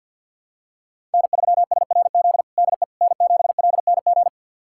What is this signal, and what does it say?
Morse code, heard as a keyed tone, about 53 wpm, 700 Hz. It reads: N3SRZ DE N7LNG
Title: USA Callsign DE Callsign - 5 Characters - 50wpm